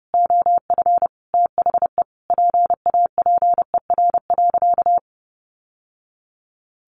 Morse code, heard as a keyed tone, about 30 wpm, 700 Hz. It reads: OF THE PAPER.
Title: Sentences from Top 1200 Words 30wpm